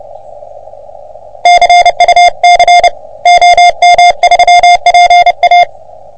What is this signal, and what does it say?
Morse code, heard as a keyed tone, about 28 wpm, 720 Hz. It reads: CUC OM3PA
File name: OM3PA PILUP CUC 388